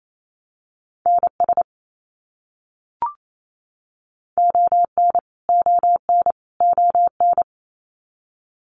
Morse code, heard as a keyed tone, about 28 wpm, 700 Hz. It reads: NS ON ON ON
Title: Canadian Provinces and Territories - Repeated 3x - 28wpm